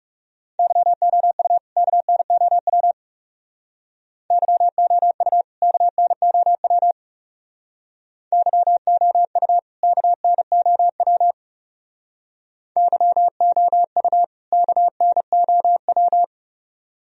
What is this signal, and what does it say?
Morse code, heard as a keyed tone, about 36 wpm, 700 Hz. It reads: YOU KNOW YOU KNOW YOU KNOW YOU KNOW
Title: Speed-Racing - Sets of 2 Words 30wpm